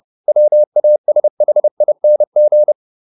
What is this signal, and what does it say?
Morse code, heard as a keyed tone, about 30 wpm, 600 Hz. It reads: WASHING